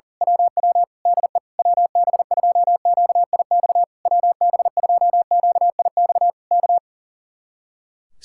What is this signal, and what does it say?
Morse code, heard as a keyed tone, about 38 wpm, 700 Hz. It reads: WW DE WB2QIX WB2QIX K
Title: Speed-Racing Common QSO Phrases Encore - 40wpm